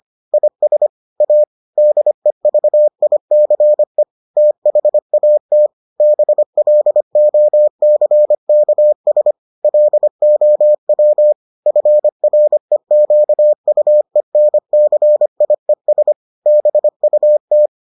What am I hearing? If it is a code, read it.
Morse code, heard as a keyed tone, about 25 wpm, 600 Hz. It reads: IS A DEVICE THAT BLOCKS LOW FREQUENCIES BUT